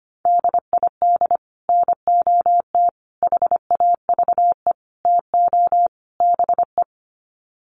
Morse code, heard as a keyed tone, about 25 wpm, 700 Hz. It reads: DID NOT HAVE TO BE
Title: Sets of 5 Words Encore2 25wpm